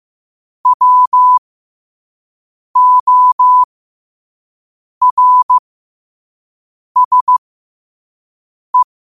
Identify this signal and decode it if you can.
Morse code, heard as a keyed tone, about 15 wpm, spaced as for 5 wpm, 1000 Hz. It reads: WORSE
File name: qotdfiles / 5wpm